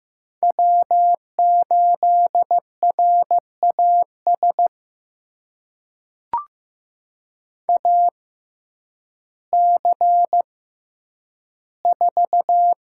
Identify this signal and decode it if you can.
Morse code, heard as a keyed tone, about 15 wpm, spaced as for 9 wpm, 700 Hz. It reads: W8RAS AC4
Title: USA Call Signs - 5 Characters - 6x Character Spacing - 15wpm